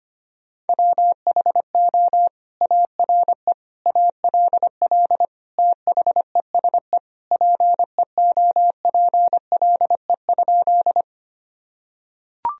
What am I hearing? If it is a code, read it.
Morse code, heard as a keyed tone, about 25 wpm, 700 Hz. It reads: WHO ARE ALL THESE PEOPLE?
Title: Sentences from Top 1000 Words - Review Part 1 - 25wpm